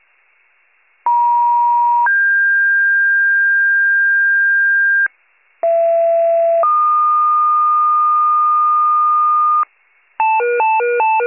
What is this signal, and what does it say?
Two-tone page, heard as a one-second tone, then a three-second tone, about 950 then 1600 Hz, about 670 then 1220 Hz, with noise. It is heard from 1.1 to 9.6 s.